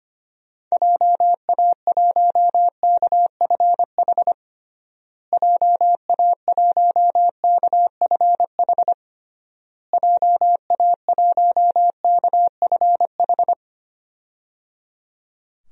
Morse code, heard as a keyed tone, about 25 wpm, 700 Hz. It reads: JA1KFH JA1KFH JA1KFH
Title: DXCC-100 Call Signs - 6 Characters - Repeated 3x - 25wpm